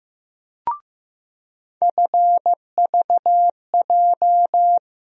Morse code, heard as a keyed tone, about 15 wpm, 700 Hz. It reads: FVJ
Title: USA Call Sign Postfixes - 3 Letters 15wpm